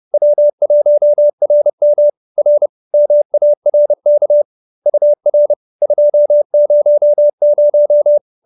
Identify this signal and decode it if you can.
Morse code, heard as a keyed tone, about 30 wpm, 570 Hz. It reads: W1RM R MARK UR 200
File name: QSO308 30